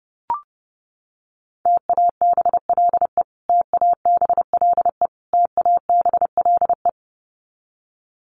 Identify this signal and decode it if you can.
Morse code, heard as a keyed tone, about 30 wpm, 700 Hz. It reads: TABLE TABLE TABLE
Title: Top 500 Words Repeated 30wpm